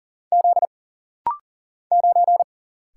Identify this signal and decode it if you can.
Morse code, heard as a keyed tone, about 40 wpm, 700 Hz. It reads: Z 9
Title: Single Letter-Number Alt - Rapid-Fire 40wpm